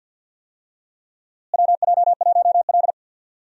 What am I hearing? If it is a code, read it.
Morse code, heard as a keyed tone, about 51 wpm, 700 Hz. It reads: WJ1L
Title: USA Call Signs - 4 Characters - Repeated 2x - 50wpm